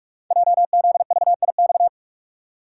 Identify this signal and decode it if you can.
Morse code, heard as a keyed tone, about 44 wpm, 700 Hz. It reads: JZVIX
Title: Code Groups - 5 Letters - 45wpm